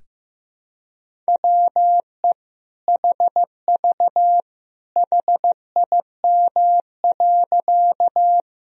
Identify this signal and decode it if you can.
Morse code, heard as a keyed tone, about 15 wpm, 700 Hz. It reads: WE HV HIM.
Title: Sentences from Top 100 Words and Abbreviations 15wpm